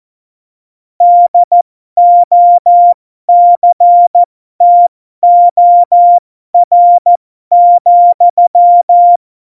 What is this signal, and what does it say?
Morse code, heard as a keyed tone, about 14 wpm, spaced as for 12 wpm, 700 Hz. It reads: DOCTOR,
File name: Morsekurs / Witze